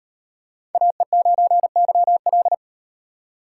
Morse code, heard as a keyed tone, about 38 wpm, 700 Hz. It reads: AE9YL